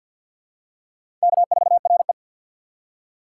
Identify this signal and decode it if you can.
Morse code, heard as a keyed tone, about 53 wpm, 700 Hz. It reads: K4RE
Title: USA Call Signs - 6x Character Spacing - 50wpm